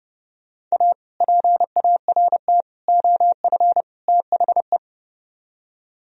Morse code, heard as a keyed tone, about 36 wpm, 690 Hz. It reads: A PART OF THE
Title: Speed-Racing - Sets of 4 Words 30wpm